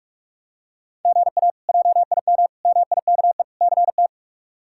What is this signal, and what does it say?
Morse code, heard as a keyed tone, about 46 wpm, 700 Hz. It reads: GA JIM MIKE CT
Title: K1USN SST QSO Practice - 45wpm